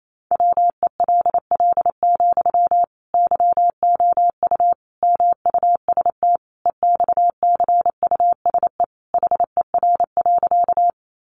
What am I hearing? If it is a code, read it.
Morse code, heard as a keyed tone, about 28 wpm, 700 Hz. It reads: WELL, YOU MUST EXCUSE HER.
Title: Sentences from Top 700 Words 28wpm